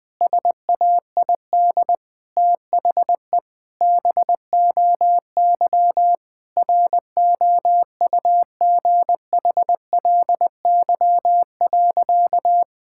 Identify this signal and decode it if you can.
Morse code, heard as a keyed tone, about 20 wpm, 700 Hz. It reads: SAID THE BOY ROUGHLY.
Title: Sentences from Top 1600 Words 20wpm